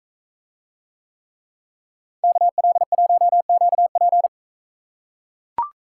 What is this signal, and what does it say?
Morse code, heard as a keyed tone, about 39 wpm, 700 Hz. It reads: KL1QP